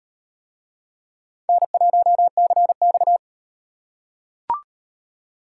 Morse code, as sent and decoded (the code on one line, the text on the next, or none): -. .---- -.-. -..-
N1CX